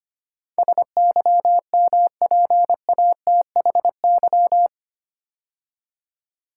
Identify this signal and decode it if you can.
Morse code, heard as a keyed tone, about 25 wpm, 700 Hz. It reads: SYMPATHY
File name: 8 Letter Words - First 3 Letters Given - 25wpm